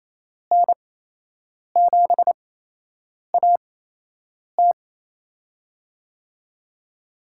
Morse code, heard as a keyed tone, about 28 wpm, spaced as for 8 wpm, 700 Hz. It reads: N7AT
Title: USA Call Signs - 5 Characters - 8x Character Spacing - 28wpm